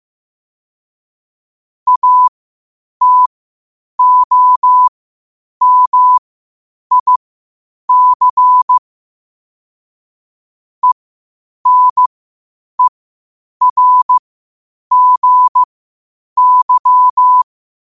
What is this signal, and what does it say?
Morse code, heard as a keyed tone, about 15 wpm, spaced as for 8 wpm, 1000 Hz. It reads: ATOMIC ENERGY
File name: qotdfiles / 8wpm